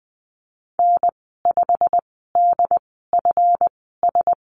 Morse code, heard as a keyed tone, about 20 wpm, spaced as for 14 wpm, 700 Hz. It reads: N5DFS
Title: 173 Lesson 28g - Number 9 - Call Signs 20wpm